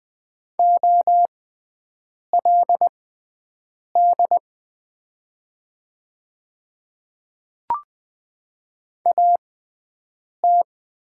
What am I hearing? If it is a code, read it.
Morse code, heard as a keyed tone, about 20 wpm, spaced as for 7 wpm, 700 Hz. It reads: OLD AT
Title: 6x Character Spacing - Top 200 Words - 20wpm